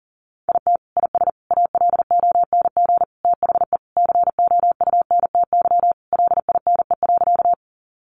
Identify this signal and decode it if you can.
Morse code, heard as a keyed tone, about 40 wpm, 700 Hz. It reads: IT IS ALONG THE COUNTY LINE.
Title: Sentences from Top 1200 Words 40wpm